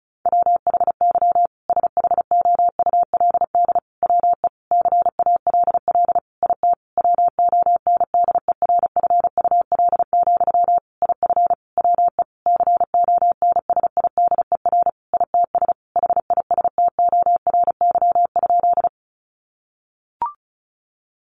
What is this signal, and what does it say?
Morse code, heard as a keyed tone, about 34 wpm, 700 Hz. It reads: WHY SHOULD WE CALL IT WONDERFUL, IF WE CONSIDER ITS HISTORY?
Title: Sentences from Top 1100 Words 35wpm